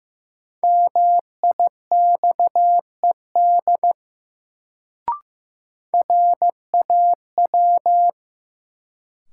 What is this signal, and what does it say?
Morse code, heard as a keyed tone, about 16 wpm, 700 Hz. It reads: MIXED RAW
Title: Sentences from Top 1500 Words - New Word Drill Alt - 15wpm